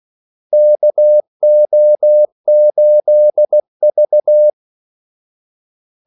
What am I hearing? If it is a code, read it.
Morse code, heard as a keyed tone, about 16 wpm, 600 Hz. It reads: KO8V
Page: Kaldesignaler 16 wpm